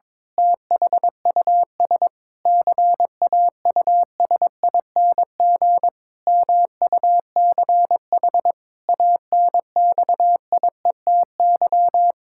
Morse code, heard as a keyed tone, about 22 wpm, 700 Hz. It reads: THUS CAUSING MUCH ANXIETY
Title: Pangram Challenge v2 - 22wpm